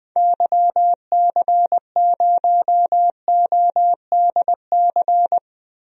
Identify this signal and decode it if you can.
Morse code, heard as a keyed tone, about 20 wpm, 700 Hz. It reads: YC0ODC
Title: DXCC-100 Call Signs - 6 Characters - 20wpm